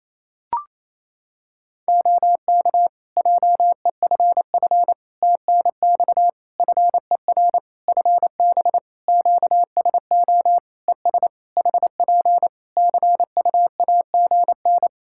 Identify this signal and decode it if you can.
Morse code, heard as a keyed tone, about 28 wpm, 700 Hz. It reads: OK JEFF TNX FER FB QSO ES HP CUAGN
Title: Practice QSO - Exchange 4 - Ending - 28wpm